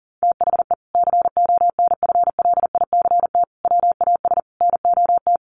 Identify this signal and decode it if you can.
Morse code, heard as a keyed tone, about 37 wpm, 700 Hz. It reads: THE CONFLICT WAS NOT
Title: Sentences from Top 1300 Words 40wpm